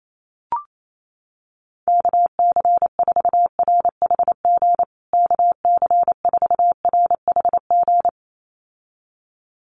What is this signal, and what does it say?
Morse code, heard as a keyed tone, about 28 wpm, 700 Hz. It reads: KC4RHG KC4RHG
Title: USA Call Signs Repeated 2x 28wpm